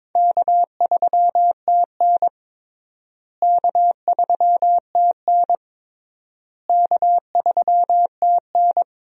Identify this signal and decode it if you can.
Morse code, heard as a keyed tone, about 21 wpm, 700 Hz. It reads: K3TN K3TN K3TN
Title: USA Call Signs - 4 Characters - Repeated 3x - 22wpm